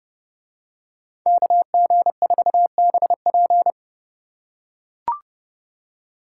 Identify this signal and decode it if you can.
Morse code, heard as a keyed tone, about 29 wpm, 700 Hz. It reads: KG4BP